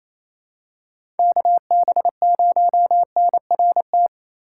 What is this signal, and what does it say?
Morse code, heard as a keyed tone, about 28 wpm, 700 Hz. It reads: KB0NRT
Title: USA Call Signs - 6 Characters - 4x Character Spacing - 28wpm